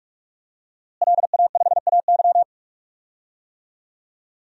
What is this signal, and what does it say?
Morse code, heard as a keyed tone, about 46 wpm, 700 Hz. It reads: LA5AY
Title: DXCC-100 Call Signs - 5 Characters - Repeated 2x - 45wpm